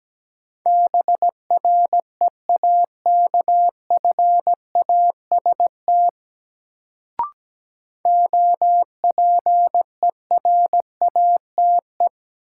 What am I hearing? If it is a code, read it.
Morse code, heard as a keyed tone, about 17 wpm, 700 Hz. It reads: BREAKFAST OPERATE
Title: Sentences from Top 1500 Words - New Word Drill Alt - 17wpm